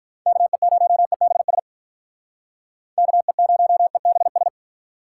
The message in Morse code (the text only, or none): KE0EBS KE0EBS